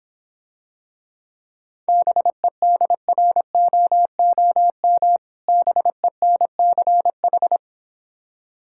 Morse code, heard as a keyed tone, about 26 wpm, 700 Hz. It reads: BEDROOM BENCH